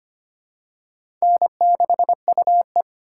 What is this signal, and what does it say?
Morse code, heard as a keyed tone, about 25 wpm, 700 Hz. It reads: N6UE
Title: USA Call Signs - 4x Character Spacing - 25wpm